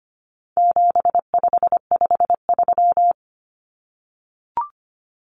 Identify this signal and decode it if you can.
Morse code, heard as a keyed tone, about 25 wpm, 700 Hz. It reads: 7553
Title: Numbers - 4-Digit 25wpm